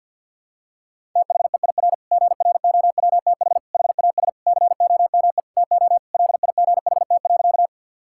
Morse code, heard as a keyed tone, about 51 wpm, 690 Hz. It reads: THEIR GROWTH HAS COME TO LIGHT.
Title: Sentences from Top 1000 Words - Review Part 5 - 50wpm